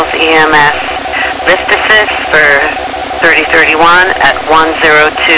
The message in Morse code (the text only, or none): none